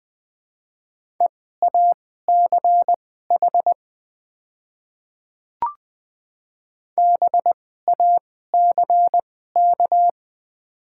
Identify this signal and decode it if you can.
Morse code, heard as a keyed tone, about 20 wpm, spaced as for 14 wpm, 700 Hz. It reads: EACH BACK
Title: 2x Character Spacing - Top 200 Words - 20wpm